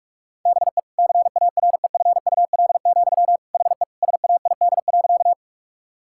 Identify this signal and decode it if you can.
Morse code, heard as a keyed tone, about 41 wpm, 700 Hz. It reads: BE CAREFUL, HE SAID.
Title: Sentences from Top 1600 Words 45wpm